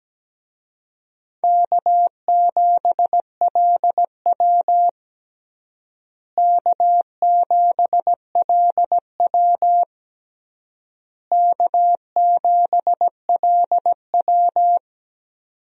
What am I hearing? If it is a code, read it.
Morse code, heard as a keyed tone, about 17 wpm, 700 Hz. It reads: K7LW K7LW K7LW
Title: USA Call Signs - 4 Characters - Repeated 3x - 17wpm